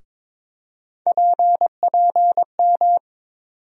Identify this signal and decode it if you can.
Morse code, heard as a keyed tone, about 22 wpm, 700 Hz. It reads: PPM
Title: USA Call Sign Postfixes - 3 Letters 22wpm